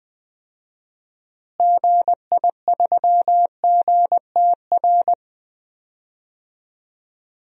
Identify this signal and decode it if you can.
Morse code, heard as a keyed tone, about 20 wpm, 700 Hz. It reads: GI3GTR
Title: DXCC-100 Call Signs - 6 Characters - Repeated 2x - 20wpm